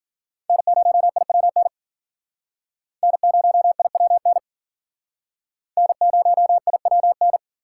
Morse code, heard as a keyed tone, about 47 wpm, 700 Hz. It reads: N0IWN N0IWN N0IWN
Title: Speed Racing - USA Call Signs 40wpm